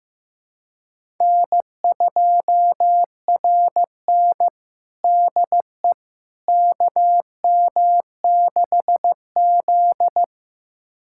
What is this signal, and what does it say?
Morse code, heard as a keyed tone, about 15 wpm, 700 Hz. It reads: N2RN DE KM6Z
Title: USA Callsign DE Callsign - 4 Characters - 15wpm